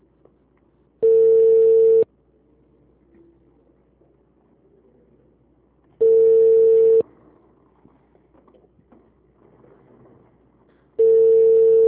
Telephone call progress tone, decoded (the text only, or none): ringback tone